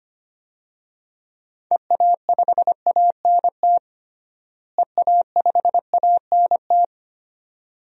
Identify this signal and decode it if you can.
Morse code, heard as a keyed tone, about 25 wpm, 700 Hz. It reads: EA5ANT EA5ANT
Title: DXCC-100 Call Signs - 6 Characters - Repeated 2x - 25wpm